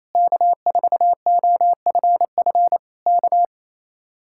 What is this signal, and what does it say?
Morse code, heard as a keyed tone, about 26 wpm, 700 Hz. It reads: K4OFF K
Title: Speed-Racing Common QSO Phrases Encore - 28wpm